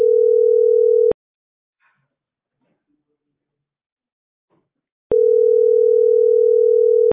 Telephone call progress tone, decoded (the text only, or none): ringback tone